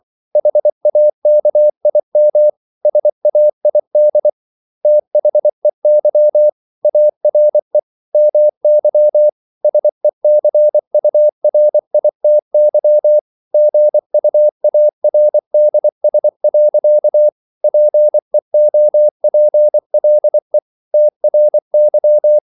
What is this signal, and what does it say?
Morse code, heard as a keyed tone, about 24 wpm, 600 Hz. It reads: HAKIM SAID THEY ARE MY SECURITY GUARDS. PEOPLE TRY